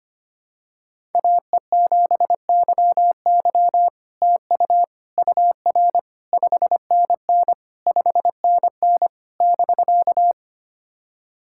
Morse code, heard as a keyed tone, about 25 wpm, 700 Hz. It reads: AE7YY TU UR 5NN 5NN <BK>
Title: SOTA QSO Practice 25wpm